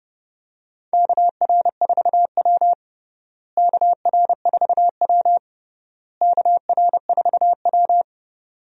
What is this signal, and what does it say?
Morse code, heard as a keyed tone, about 32 wpm, 700 Hz. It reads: KR4W KR4W KR4W
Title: USA Call Signs - 4 Characters - Repeated 3x - 30wpm